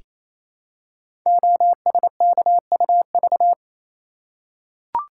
Morse code, heard as a keyed tone, about 29 wpm, 700 Hz. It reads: OSKUV
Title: Code Groups - 5 Letters - 28wpm